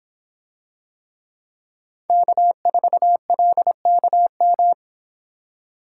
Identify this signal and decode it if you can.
Morse code, heard as a keyed tone, about 25 wpm, 700 Hz. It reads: K4LKM